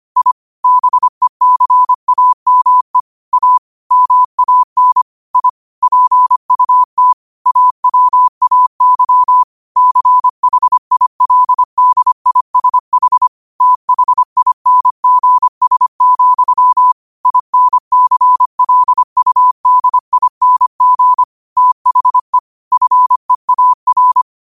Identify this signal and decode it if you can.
Morse code, heard as a keyed tone, about 25 wpm, 1000 Hz. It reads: I BECAME A MAN I PUT AWAY CHILDISH THINGS, INCLUDING THE FEAR
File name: qotdfiles / 25wpm